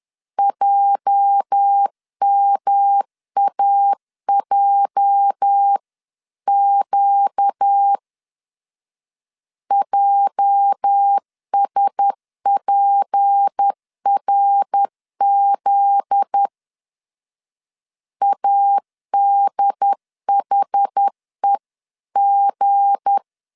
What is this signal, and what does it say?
Morse code, heard as a keyed tone, about 10 wpm, 800 Hz. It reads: JMAJQ JSPRZ ADHEG
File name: LEZIONI / 8 YJ45 / 8 YJ 10wpm 2651